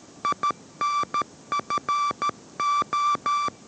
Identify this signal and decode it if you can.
Morse code, heard as a keyed tone, about 13 wpm, 1170 Hz. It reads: INFO